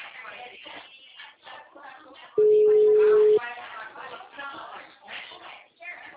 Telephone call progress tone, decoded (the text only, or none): ringback tone